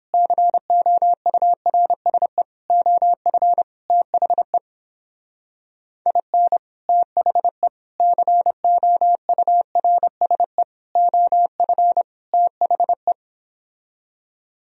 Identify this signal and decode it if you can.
Morse code, heard as a keyed tone, about 31 wpm, 700 Hz. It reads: COURSE OF THE IN THE COURSE OF THE